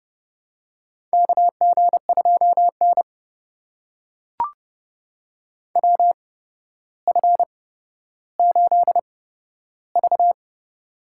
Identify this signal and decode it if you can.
Morse code, heard as a keyed tone, about 30 wpm, spaced as for 13 wpm, 700 Hz. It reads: KG2N WF8V
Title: USA Call Signs - 4 Characters - 8x Character Spacing - 30wpm